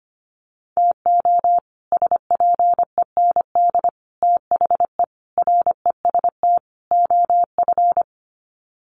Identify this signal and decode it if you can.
Morse code, heard as a keyed tone, about 25 wpm, 700 Hz. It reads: TO SPEND THE REST OF
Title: Sets of 5 Words Encore2 25wpm